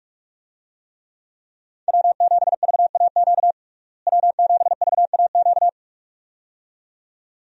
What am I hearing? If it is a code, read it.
Morse code, heard as a keyed tone, about 47 wpm, 700 Hz. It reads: W7VAQ W7VAQ
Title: USA Call Signs - 5 Characters - Repeated 2x - 45wpm